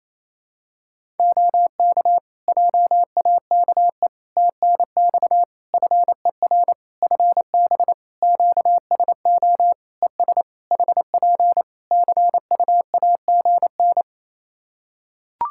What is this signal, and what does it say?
Morse code, heard as a keyed tone, about 28 wpm, 700 Hz. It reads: OK JAKE TNX FER FB QSO ES HP CUAGN
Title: Practice QSO - Exchange 4 - Ending - 28wpm